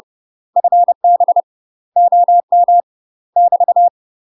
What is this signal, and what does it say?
Morse code, heard as a keyed tone, about 30 wpm, 700 Hz. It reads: FB OM <BT>
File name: UK MASTER QSOs TRAINING/30WPM